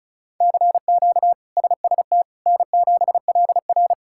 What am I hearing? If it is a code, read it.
Morse code, heard as a keyed tone, about 34 wpm, 700 Hz. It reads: CQ SST N7LR
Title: K1USN SST QSO Practice - 35wpm